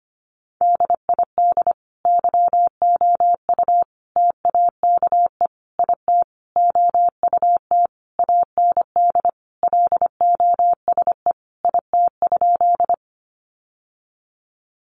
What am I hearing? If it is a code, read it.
Morse code, heard as a keyed tone, about 25 wpm, 700 Hz. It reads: DID YOU TAKE IT OUT AND LOSE IT?
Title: Sentences from Top 1100 Words 25wpm